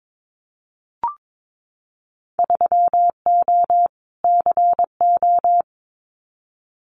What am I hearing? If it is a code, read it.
Morse code, heard as a keyed tone, about 22 wpm, 700 Hz. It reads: 3O CO
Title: Winter Field Day Exchanges Encore 22wpm